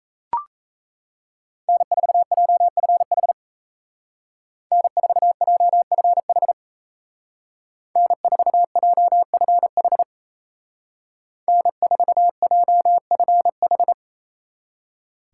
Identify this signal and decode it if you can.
Morse code, heard as a keyed tone, about 37 wpm, 700 Hz. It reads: N4JFH N4JFH N4JFH N4JFH